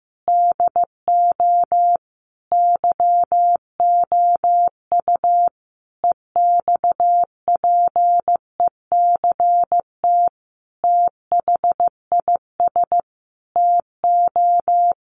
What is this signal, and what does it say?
Morse code, heard as a keyed tone, about 15 wpm, 700 Hz. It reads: DO YOU EXPECT THIS TO
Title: Sentences from Top 1100 Words 15wpm